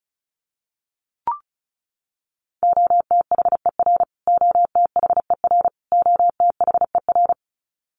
Morse code, heard as a keyed tone, about 34 wpm, 700 Hz. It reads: OTHER OTHER OTHER
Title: Top 200 Words Repeated 35wpm